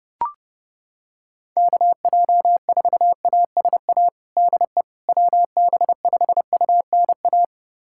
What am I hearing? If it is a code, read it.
Morse code, heard as a keyed tone, about 30 wpm, 700 Hz. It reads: KJ4ASA DE WB5UNA
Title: USA Callsign DE Callsign - 6 Characters - 30wpm